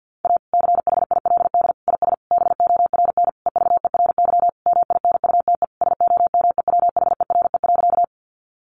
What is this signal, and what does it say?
Morse code, heard as a keyed tone, about 50 wpm, 700 Hz. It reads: A CHILD IS BORN EVERY MINUTE SOMEWHERE.
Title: Sentences from Top 1400 Words 50wpm